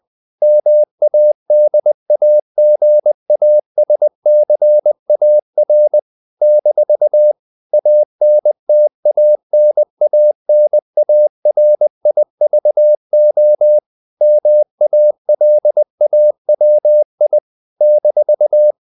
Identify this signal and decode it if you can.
Morse code, heard as a keyed tone, about 20 wpm, 600 Hz. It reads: MADAGASCAR - ANTANANARIVO MALAWI -
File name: CW / Practice / NationalCapitals20